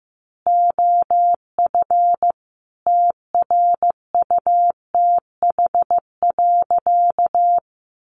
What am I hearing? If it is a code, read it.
Morse code, heard as a keyed tone, about 14 wpm, 700 Hz. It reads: OF TRUTH.
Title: Sentences from Top 1100 Words 15wpm